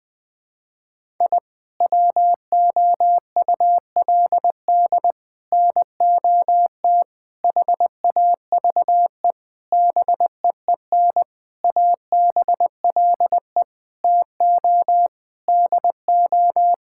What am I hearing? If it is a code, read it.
Morse code, heard as a keyed tone, about 20 wpm, 700 Hz. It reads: I WOULD NOT HAVE BEEN ABLE TO DO
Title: Sentences with 6-Word Phrases - 20wpm